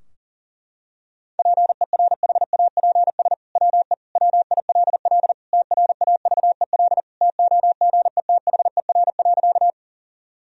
Morse code, heard as a keyed tone, about 40 wpm, 700 Hz. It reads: PERHAPS WE WILL TRAVEL TOGETHER.
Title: Sentences from Top 1000 Words - Review Part 2 - 40wpm